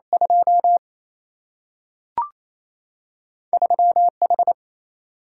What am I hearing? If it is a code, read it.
Morse code, heard as a keyed tone, about 28 wpm, 700 Hz. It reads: 2 3H
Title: Single-Letter Single-Number 28wpm